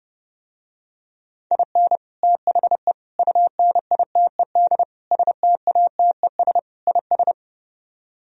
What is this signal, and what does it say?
Morse code, heard as a keyed tone, about 30 wpm, 700 Hz. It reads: IN THE UNITED STATES IS